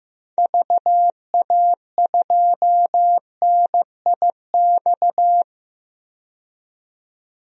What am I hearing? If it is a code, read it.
Morse code, heard as a keyed tone, about 15 wpm, 700 Hz. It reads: VA2NIX
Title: DXCC-100 Call Signs - 6 Characters - 15wpm